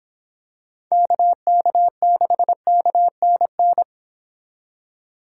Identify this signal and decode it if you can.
Morse code, heard as a keyed tone, about 26 wpm, 700 Hz. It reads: KK6KNN
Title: Speed Racing - USA 2x3 Call Signs 22wpm